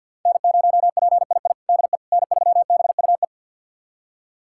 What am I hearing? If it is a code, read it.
Morse code, heard as a keyed tone, about 50 wpm, 700 Hz. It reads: N0PII DE N3BUE
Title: USA Callsign DE Callsign - 5 Characters - 50wpm